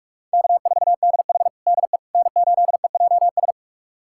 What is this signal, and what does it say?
Morse code, heard as a keyed tone, about 43 wpm, 700 Hz. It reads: K4DH DE N8EJS
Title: USA Callsign DE Callsign 45wpm